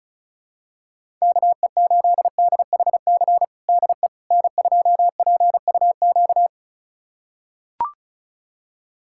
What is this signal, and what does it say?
Morse code, heard as a keyed tone, about 34 wpm, 700 Hz. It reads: KE8DHC DE N2PUQ
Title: USA Callsign DE Callsign 35wpm